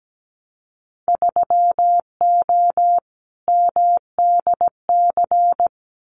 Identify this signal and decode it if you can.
Morse code, heard as a keyed tone, about 17 wpm, 700 Hz. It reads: 3O MDC
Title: Winter Field Day Exchanges Encore 17wpm